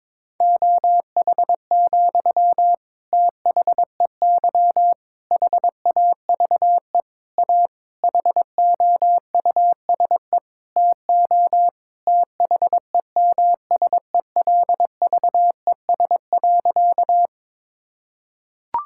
Morse code, heard as a keyed tone, about 22 wpm, 700 Hz. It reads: OH, THEY HAVE A HOUSE TO THEMSELVES.
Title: Sentences from Top 1000 Words - Review Part 5 - 22wpm